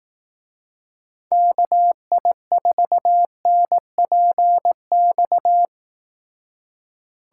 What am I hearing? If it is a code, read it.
Morse code, heard as a keyed tone, about 18 wpm, 700 Hz. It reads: KI4NPX
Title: Speed Racing - USA 2x3 Call Signs 15wpm